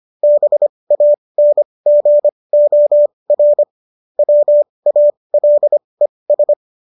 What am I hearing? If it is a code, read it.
Morse code, heard as a keyed tone, about 25 wpm, spaced as for 20 wpm, 590 Hz. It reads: BANGOR WALES